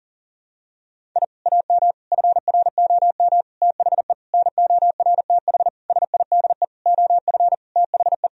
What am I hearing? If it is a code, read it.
Morse code, heard as a keyed tone, about 40 wpm, 700 Hz. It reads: I AM FROM THE NORTH SIDE OF THE
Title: Sentences from Top 1000 Words - Review Part 1 - 40wpm